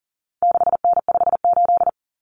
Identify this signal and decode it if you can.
Morse code, heard as a keyed tone, about 39 wpm, 700 Hz. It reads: 6N58
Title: Common Cut Numbers - 4-Digit 40wpm